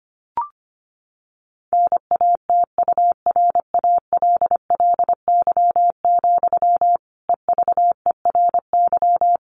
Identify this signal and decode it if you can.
Morse code, heard as a keyed tone, about 25 wpm, 700 Hz. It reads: NATURALLY, EVERY
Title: Sentences from Top 1500 Words 25wpm